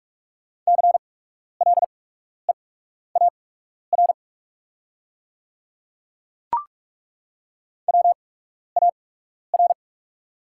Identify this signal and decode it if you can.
Morse code, heard as a keyed tone, about 42 wpm, spaced as for 12 wpm, 690 Hz. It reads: CLEAR WAR
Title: 8x Character Spacing - Top 500 Words - 45wpm